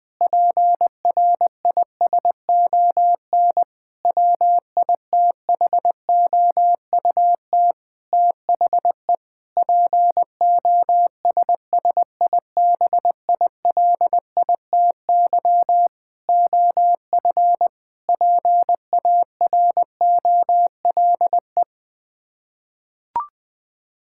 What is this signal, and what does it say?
Morse code, heard as a keyed tone, about 21 wpm, 700 Hz. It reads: PRISON WITHOUT THE POSSIBILITY OF PAROLE
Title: Sets of 8 Words - 20wpm